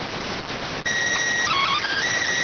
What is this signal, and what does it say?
SSTV (slow-scan television), PD 290.